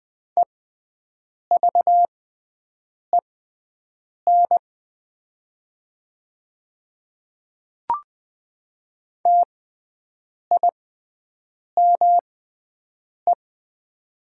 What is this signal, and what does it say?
Morse code, heard as a keyed tone, about 20 wpm, spaced as for 7 wpm, 700 Hz. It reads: EVEN TIME
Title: 6x Character Spacing - Top 200 Words - 20wpm